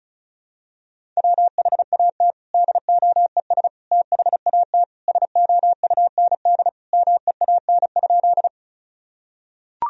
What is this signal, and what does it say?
Morse code, heard as a keyed tone, about 36 wpm, 700 Hz. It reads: WHAT DOES THAT SOUND MEAN?
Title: Sentences from Top 1000 Words - Review Part 1 - 35wpm